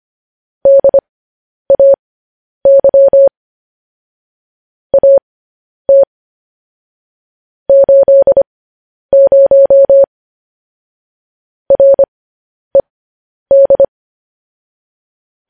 Morse code, heard as a keyed tone, about 25 wpm, spaced as for 10 wpm, 550 Hz. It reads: DAY AT 80 RED
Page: news-headlines-in-morse-code-at-10-wpm